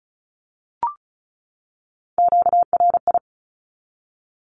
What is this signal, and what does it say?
Morse code, heard as a keyed tone, about 34 wpm, 700 Hz. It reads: QRI
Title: Q-Codes 35wpm